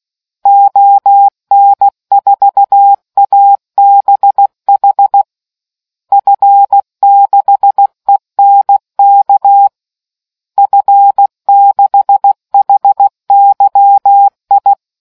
Morse code, heard as a keyed tone, about 16 wpm, 800 Hz. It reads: ON4ABH F6ENK F6HYI